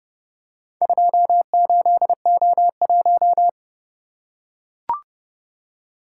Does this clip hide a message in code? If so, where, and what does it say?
Yes: Morse code from 0.8 to 3.5 s, heard as a keyed tone, about 29 wpm, 700 Hz. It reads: 28O1